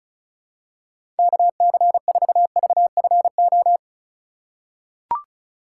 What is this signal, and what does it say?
Morse code, heard as a keyed tone, about 34 wpm, 700 Hz. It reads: KC4VFO
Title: USA Call Signs - 2x Character Spacing - 35wpm